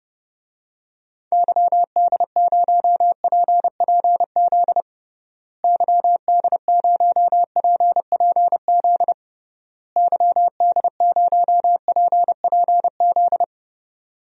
Morse code, heard as a keyed tone, about 30 wpm, 700 Hz. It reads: YD0PPZ YD0PPZ YD0PPZ
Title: DXCC-100 Call Signs - 6 Characters - Repeated 3x - 30wpm